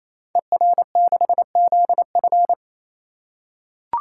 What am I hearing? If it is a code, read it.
Morse code, heard as a keyed tone, about 27 wpm, 700 Hz. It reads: ER6ZF
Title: Code Groups - 5 Letter:Numbers - 28wpm